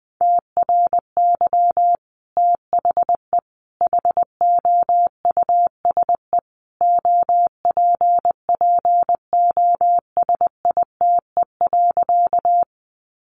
Morse code, heard as a keyed tone, about 20 wpm, 700 Hz. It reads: TRY THE HOUSE OPPOSITE.
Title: Sentences from Top 1300 Words 20wpm